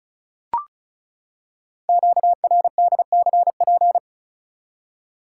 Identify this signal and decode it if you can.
Morse code, heard as a keyed tone, about 36 wpm, 700 Hz. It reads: QRDCP
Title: Code Groups - 5 Letter:Numbers - 35wpm